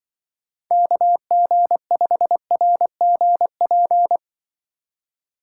Morse code, heard as a keyed tone, about 24 wpm, 700 Hz. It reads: KG5RGP
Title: Speed Racing - USA 2x3 Call Signs 20wpm